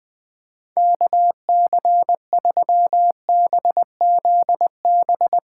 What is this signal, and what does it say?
Morse code, heard as a keyed tone, about 20 wpm, 700 Hz. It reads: KC3BZB
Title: USA Call Signs - 6 Characters - 6x Character Spacing - 20wpm